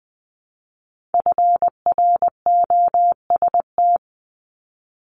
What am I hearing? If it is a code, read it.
Morse code, heard as a keyed tone, about 20 wpm, 700 Hz. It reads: FROST
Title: Multi-letter Phonemes 20wpm